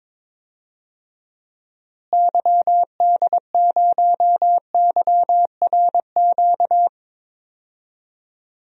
Morse code, heard as a keyed tone, about 22 wpm, 700 Hz. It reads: YD0YRQ
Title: DXCC-100 Call Signs - 6 Characters - Repeated 3x - 22wpm